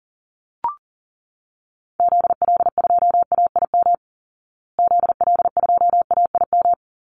Morse code, heard as a keyed tone, about 37 wpm, 700 Hz. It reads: ZL2AIM ZL2AIM
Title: Call Signs Repeated 3x - 40wpm